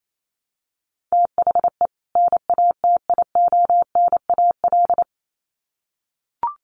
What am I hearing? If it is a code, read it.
Morse code, heard as a keyed tone, about 28 wpm, 700 Hz. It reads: THE NATIONAL
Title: Sets of 2 Words 28wpm